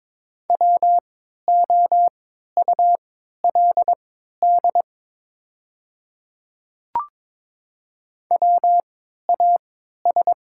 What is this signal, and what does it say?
Morse code, heard as a keyed tone, about 22 wpm, spaced as for 12 wpm, 700 Hz. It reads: WOULD WAS
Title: 3x Character Spacing - Top 100 Words - 22wpm